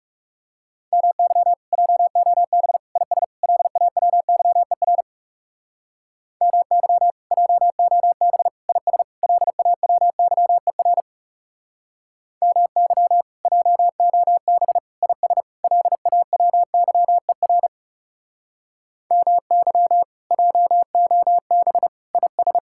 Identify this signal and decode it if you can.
Morse code, heard as a keyed tone, about 38 wpm, 700 Hz. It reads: MY JOB IS LAWYER MY JOB IS LAWYER MY JOB IS LAWYER MY JOB IS